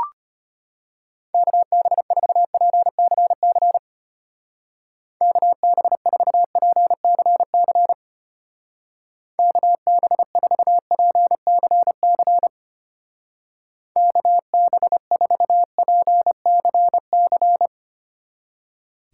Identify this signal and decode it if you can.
Morse code, heard as a keyed tone, about 32 wpm, 700 Hz. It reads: KB4PCC KB4PCC KB4PCC KB4PCC